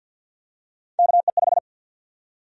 Morse code, heard as a keyed tone, about 52 wpm, 700 Hz. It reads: KE5
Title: USA Call Sign Prefixes - 50wpm